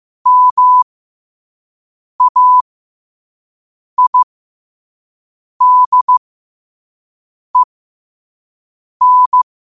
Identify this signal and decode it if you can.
Morse code, heard as a keyed tone, about 15 wpm, spaced as for 5 wpm, 1000 Hz. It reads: MAIDEN
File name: qotdfiles / 5wpm